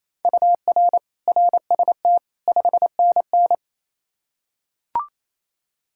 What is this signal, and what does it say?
Morse code, heard as a keyed tone, about 28 wpm, 700 Hz. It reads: UR RST 5NN
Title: Practice QSO - Exchange 1 - Beginning - 28wpm